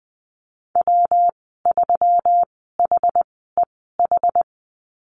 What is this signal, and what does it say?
Morse code, heard as a keyed tone, about 20 wpm, spaced as for 14 wpm, 700 Hz. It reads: W3HEH
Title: 154 Lesson 25g - Letter p - Call Signs 20wpm